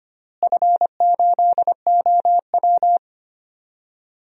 Morse code, heard as a keyed tone, about 25 wpm, 700 Hz. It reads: F8OW
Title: DXCC-100 Call Signs - 4 Characters - Repeated 3x - 25wpm